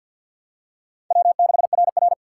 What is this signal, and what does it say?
Morse code, heard as a keyed tone, about 46 wpm, 700 Hz. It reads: W6RR
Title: USA Call Signs - 4 Characters - 3x Character Spacing - 50wpm